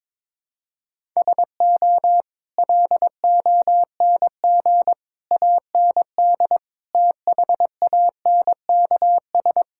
Morse code, heard as a keyed tone, about 22 wpm, 700 Hz. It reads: SO LONG AND THANKS
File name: practice / 22wpm / Sentences with 8-Word Phrases - 22wpm